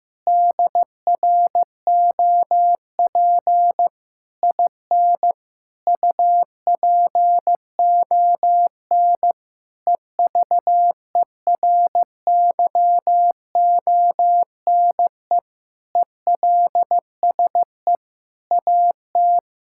Morse code, heard as a keyed tone, about 15 wpm, 700 Hz. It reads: DROP IN UPON EVERYONE ELSE AT
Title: Sentences from Top 1600 Words 15wpm